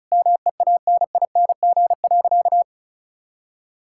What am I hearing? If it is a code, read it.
Morse code, heard as a keyed tone, about 34 wpm, 700 Hz. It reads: MEANING.
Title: Sentences from Top 1000 Words - Review Part 6 - 35wpm